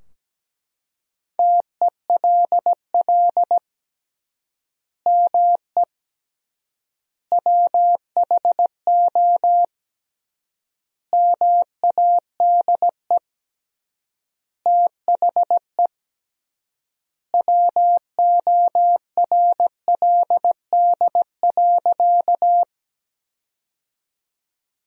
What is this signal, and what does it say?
Morse code, heard as a keyed tone, about 17 wpm, 700 Hz. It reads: TELL ME WHO MADE THE WORLD.
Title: 3x Word-Spacing - Sentences from Top 200 Words - 17wpm